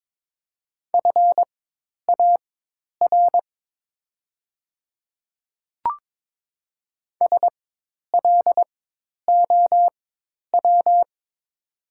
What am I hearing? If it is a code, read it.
Morse code, heard as a keyed tone, about 22 wpm, spaced as for 10 wpm, 700 Hz. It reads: FAR SLOW
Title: 4x Character Spacing - Top 500 Words - 22wpm